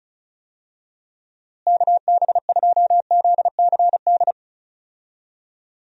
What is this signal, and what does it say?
Morse code, heard as a keyed tone, about 34 wpm, 700 Hz. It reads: KB2ZCD
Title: USA Call Signs - 6 Characters - Repeated 2x - 35wpm